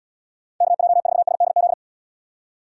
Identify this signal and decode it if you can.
Morse code, heard as a keyed tone, about 78 wpm, 700 Hz. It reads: KJ4SDZ